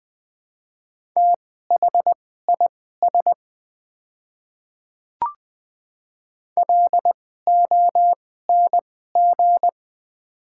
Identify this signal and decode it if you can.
Morse code, heard as a keyed tone, about 20 wpm, spaced as for 15 wpm, 690 Hz. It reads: THIS LONG
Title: 2x Character Spacing - Top 100 Words - 20wpm